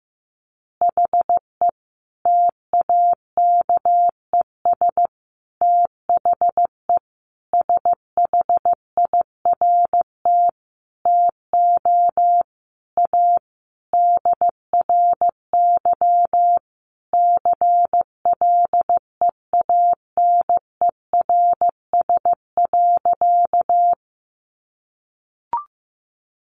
Morse code, heard as a keyed tone, about 15 wpm, 700 Hz. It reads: HE TAKES THE SHIRT TO A DRY CLEANERS.
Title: Additional Easy Phrases 15wpm